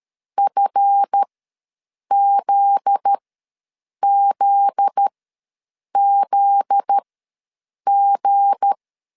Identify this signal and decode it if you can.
Morse code, heard as a keyed tone, about 13 wpm, spaced as for 7 wpm, 800 Hz. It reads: FZZZG
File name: 7 GF9Z ws 12wpm